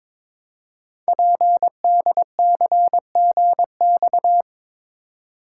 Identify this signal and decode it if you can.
Morse code, heard as a keyed tone, about 22 wpm, 700 Hz. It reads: PDCGX
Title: Code Groups - 5 Letters - 22wpm